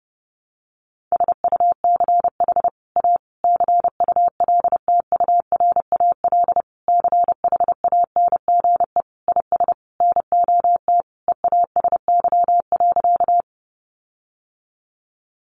Morse code, heard as a keyed tone, about 30 wpm, 700 Hz. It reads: SUCH A CULTURAL CHANGE IS NOT EASY.
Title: Sentences from Top 1500 Words 30wpm